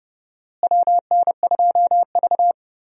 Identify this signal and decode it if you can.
Morse code, heard as a keyed tone, about 30 wpm, 700 Hz. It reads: WN2V
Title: USA Call Signs - 4 Characters - Repeated 3x - 30wpm